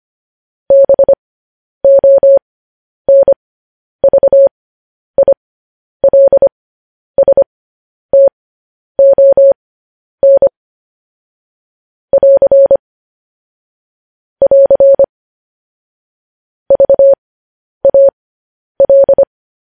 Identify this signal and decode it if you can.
Morse code, heard as a keyed tone, about 25 wpm, spaced as for 10 wpm, 550 Hz. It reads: BONVILSTON <AR> <AR> VAL
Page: news headlines in morse code at 10 wpm